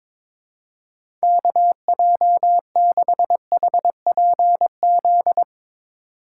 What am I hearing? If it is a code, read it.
Morse code, heard as a keyed tone, about 22 wpm, 700 Hz. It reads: KJ6HPZ
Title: USA Call Signs - 6 Characters - 3x Character Spacing - 22wpm